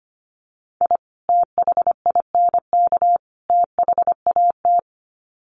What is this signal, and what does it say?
Morse code, heard as a keyed tone, about 25 wpm, 700 Hz. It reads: I THINK THAT
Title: Sets of 3 Words 25wpm